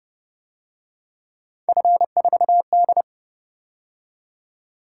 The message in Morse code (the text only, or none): F4D